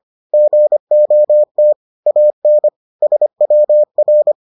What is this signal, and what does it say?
Morse code, heard as a keyed tone, about 25 wpm, 600 Hz. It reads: GOT AN SWR